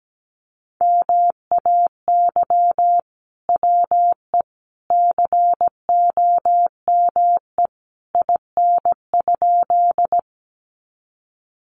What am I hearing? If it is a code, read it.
Morse code, heard as a keyed tone, about 17 wpm, 700 Hz. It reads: MAY WE COME IN?
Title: Sentences from Top 100 Words 17wpm